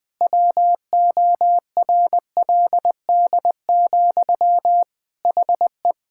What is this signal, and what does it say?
Morse code, heard as a keyed tone, about 20 wpm, 700 Hz. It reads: WORLD, HE
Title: Sentences from Top 1600 Words 20wpm